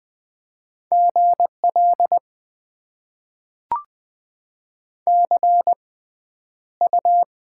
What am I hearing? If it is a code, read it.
Morse code, heard as a keyed tone, about 20 wpm, spaced as for 10 wpm, 700 Hz. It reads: GL CU
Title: Top 90 QSO Elements - 6x Character Spacing - 20wpm